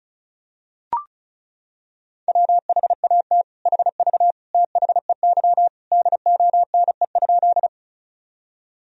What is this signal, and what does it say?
Morse code, heard as a keyed tone, about 36 wpm, 700 Hz. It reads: WHAT HV THEY DONE?
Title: Sentences from Top 500 Words and Abbreviations 35wpm